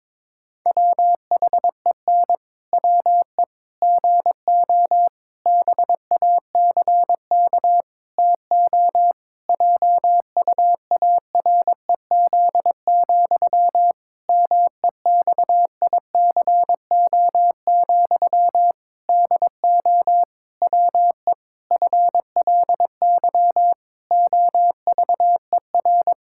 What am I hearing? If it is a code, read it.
Morse code, heard as a keyed tone, about 22 wpm, 700 Hz. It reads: WHEN WE GO BACK TO JUAREZ, MEXICO, DO WE FLY OVER
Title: Pangram Challenge v2 - 22wpm